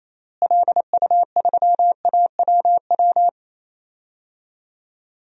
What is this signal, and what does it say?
Morse code, heard as a keyed tone, about 28 wpm, 700 Hz. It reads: LU3AWW
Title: DXCC-100 Call Signs - 6 Characters - Repeated 2x - 28wpm